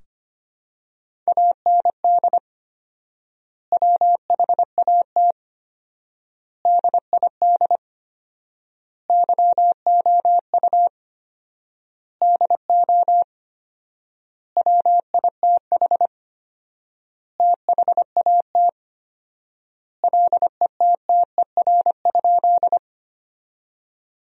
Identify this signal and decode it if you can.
Morse code, heard as a keyed tone, about 25 wpm, 700 Hz. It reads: AND WHAT DID YOU DO WITH THAT LETTER?
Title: 4x Word-Spacing - Sentences from Top 300 Words - 25wpm